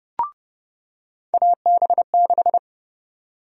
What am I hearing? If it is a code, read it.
Morse code, heard as a keyed tone, about 30 wpm, 700 Hz. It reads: AB6
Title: USA Call Sign Prefixes - 30wpm